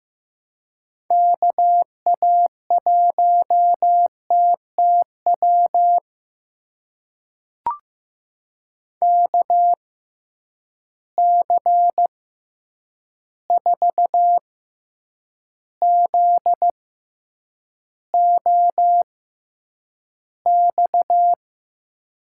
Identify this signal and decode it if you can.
Morse code, heard as a keyed tone, about 15 wpm, spaced as for 8 wpm, 700 Hz. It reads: KA1TTW KC4ZOX
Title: USA Call Signs - 6 Characters - 6x Character Spacing - 15wpm